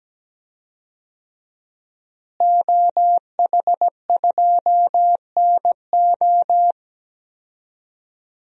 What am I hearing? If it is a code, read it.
Morse code, heard as a keyed tone, about 17 wpm, 700 Hz. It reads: OH2NO
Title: DXCC-100 Call Signs - 5 Characters - Repeated 2x - 17wpm